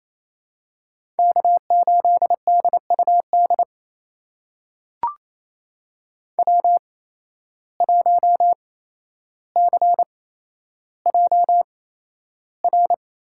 Morse code, heard as a keyed tone, about 28 wpm, spaced as for 12 wpm, 700 Hz. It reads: K8DUD W1CJR
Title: USA Call Signs - 5 Characters - 8x Character Spacing - 28wpm